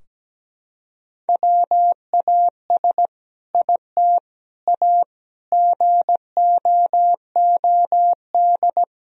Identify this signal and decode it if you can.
Morse code, heard as a keyed tone, about 17 wpm, 700 Hz. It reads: WAS IT A GOOD
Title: Sentences from Top 1000 Words - Review Part 4 - 17wpm